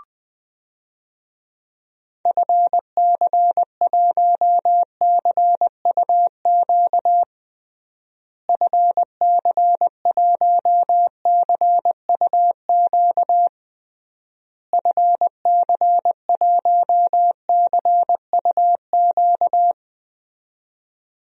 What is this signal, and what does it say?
Morse code, heard as a keyed tone, about 20 wpm, 700 Hz. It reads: FC1CUQ FC1CUQ FC1CUQ
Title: DXCC-100 Call Signs - 6 Characters - Repeated 3x - 20wpm